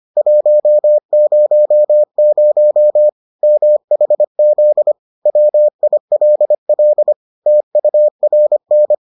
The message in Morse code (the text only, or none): 100 MHZ WILL TURN